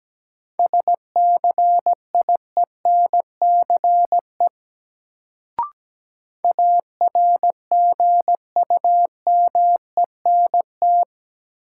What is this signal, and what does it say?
Morse code, heard as a keyed tone, about 17 wpm, 700 Hz. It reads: SCIENCE ARGUMENT
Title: Sentences from Top 1400 Words - New Word Drill Alt - 17wpm